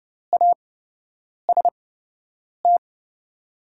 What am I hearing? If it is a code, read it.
Morse code, heard as a keyed tone, about 30 wpm, spaced as for 8 wpm, 690 Hz. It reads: AST
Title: 8x Character Spacing - Top 500 Words - 30wpm